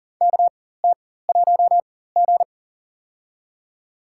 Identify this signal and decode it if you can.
Morse code, heard as a keyed tone, about 40 wpm, spaced as for 19 wpm, 700 Hz. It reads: KT1G
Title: USA Call Signs - 4 Characters - 4x Character Spacing - 40wpm